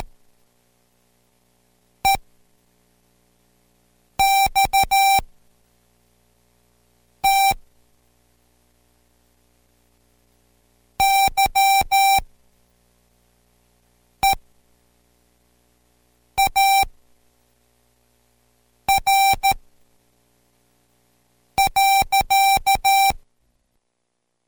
Morse code, heard as a keyed tone, about 13 wpm, spaced as for 4 wpm, 780 Hz. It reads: EXT YEAR.